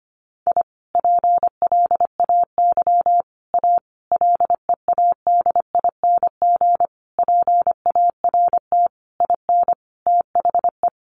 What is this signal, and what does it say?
Morse code, heard as a keyed tone, about 25 wpm, 700 Hz. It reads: I PLAY A LEADING PART IN THE
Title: Sentences from Top 1100 Words 25wpm